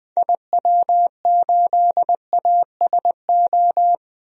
Morse code, heard as a keyed tone, about 20 wpm, 700 Hz. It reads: IW8ASO
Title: DXCC-100 Call Signs - 6 Characters - Repeated 2x - 20wpm